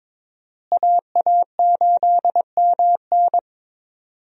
Morse code, heard as a keyed tone, about 22 wpm, 700 Hz. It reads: AA8MN
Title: USA Call Signs - 5 Characters - Repeated 3x - 22wpm